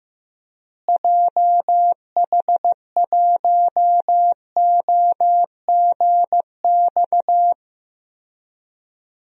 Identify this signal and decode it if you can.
Morse code, heard as a keyed tone, about 15 wpm, 700 Hz. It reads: JH1OGX
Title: DXCC-100 Call Signs - 6 Characters - Repeated 3x - 15wpm